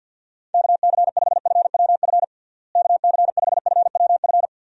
Morse code, heard as a keyed tone, about 50 wpm, 700 Hz. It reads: KC5FWF KC5FWF
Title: USA Call Signs - 6 Characters - Repeated 2x - 50wpm